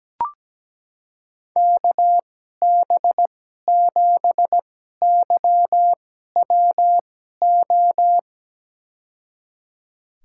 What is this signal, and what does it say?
Morse code, heard as a keyed tone, about 17 wpm, spaced as for 12 wpm, 700 Hz. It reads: KB7YWO
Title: USA Call Signs - 2x Character Spacing - 17wpm